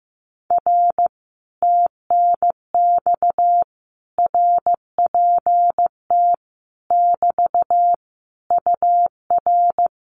Morse code, heard as a keyed tone, about 15 wpm, 700 Hz. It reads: R TNX RPT <BT> UR
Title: Common QSO Phrases Encore 15wpm